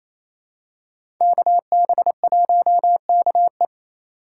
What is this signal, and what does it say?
Morse code, heard as a keyed tone, about 28 wpm, 700 Hz. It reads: KB1KE
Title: USA Call Signs - 5 Characters - 6x Character Spacing - 28wpm